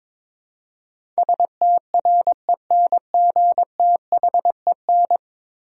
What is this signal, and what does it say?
Morse code, heard as a keyed tone, about 22 wpm, 700 Hz. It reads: STRENGTHEN
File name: practice / 22wpm / 10 Letter Words - First 3 Letters Given - 22wpm